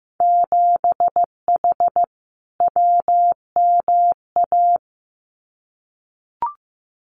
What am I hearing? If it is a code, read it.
Morse code, heard as a keyed tone, about 15 wpm, 700 Hz. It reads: 7H WMA
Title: Winter Field Day Exchanges Encore 15wpm